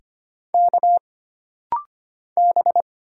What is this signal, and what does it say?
Morse code, heard as a keyed tone, about 25 wpm, 700 Hz. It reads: K B
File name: Single Character (Letter:Number:Punctuation:Prosign) - Rapid-Fire - 25wpm